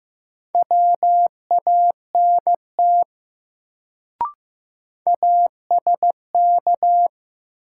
Morse code, heard as a keyed tone, about 15 wpm, 700 Hz. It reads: WANT ASK
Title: Sentences from Top 200 Words - New Word Drill - 15wpm